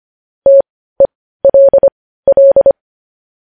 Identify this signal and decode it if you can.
Morse code, heard as a keyed tone, about 25 wpm, spaced as for 15 wpm, 550 Hz. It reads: TELL
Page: news-headlines-in-morse-code-at-15-wpm